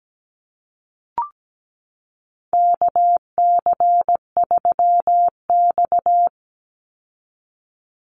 Morse code, heard as a keyed tone, about 17 wpm, 700 Hz. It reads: KC3X
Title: Call Signs Repeated 2x - 17wpm